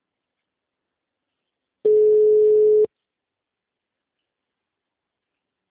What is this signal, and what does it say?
Ringback tone (Europe, 425 Hz).
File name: out- -2012- - - .3561